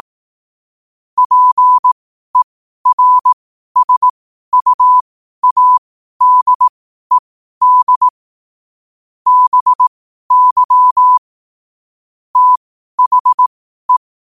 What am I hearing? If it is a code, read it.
Morse code, heard as a keyed tone, about 18 wpm, spaced as for 12 wpm, 1000 Hz. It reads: PERSUADED BY THE
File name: qotdfiles / 12wpm